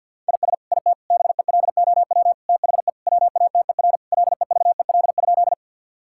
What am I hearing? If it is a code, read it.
Morse code, heard as a keyed tone, about 50 wpm, 700 Hz. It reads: IS IT BELOW THE WATER LEVEL?
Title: Sentences from Top 1000 Words - Review Part 5 - 50wpm